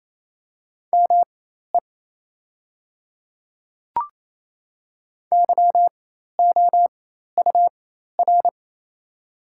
Morse code, heard as a keyed tone, about 28 wpm, spaced as for 13 wpm, 700 Hz. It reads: ME YOUR
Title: 4x Character Spacing - Top 300 Words - 28wpm